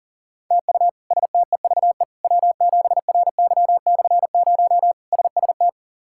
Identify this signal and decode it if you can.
Morse code, heard as a keyed tone, about 40 wpm, 700 Hz. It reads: TU STEVE W7RY/0 SST
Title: K1USN SST QSO Practice - 40wpm